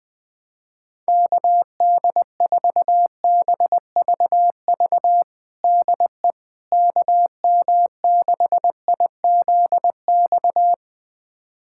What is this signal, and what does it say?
Morse code, heard as a keyed tone, about 20 wpm, 700 Hz. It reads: KD4BVV DE KM6IZX
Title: USA Callsign DE Callsign - 6 Characters - 20wpm